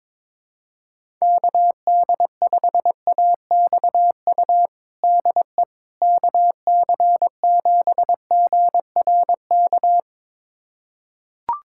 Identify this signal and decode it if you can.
Morse code, heard as a keyed tone, about 22 wpm, 700 Hz. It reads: KD5AXU DE KC7GRK
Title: USA Callsign DE Callsign - 6 Characters - 22wpm